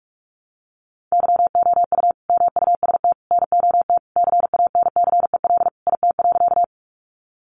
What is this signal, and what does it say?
Morse code, heard as a keyed tone, about 44 wpm, 700 Hz. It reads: YOU MUST NOT CANCEL IT.
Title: Sentences from Top 1200 Words 45wpm